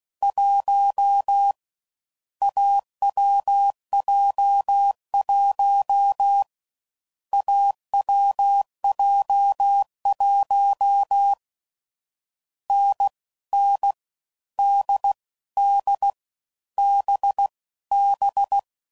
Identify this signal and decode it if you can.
Morse code, heard as a keyed tone, about 16 wpm, 800 Hz. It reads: 1 AWJ1 AWJ1 N N D D B B
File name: Combinations / Difficult Combos / awj1 ndb6 gz7